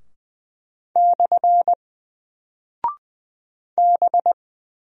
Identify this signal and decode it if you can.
Morse code, heard as a keyed tone, about 20 wpm, 700 Hz. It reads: / B
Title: Single Character (Letter:Number:Punctuation:Prosign) 20wpm